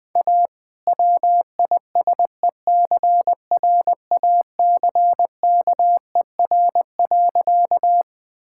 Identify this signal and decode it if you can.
Morse code, heard as a keyed tone, about 21 wpm, 700 Hz. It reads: A WISECRACKER.
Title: All-Age Appropriate Jokes - 20wpm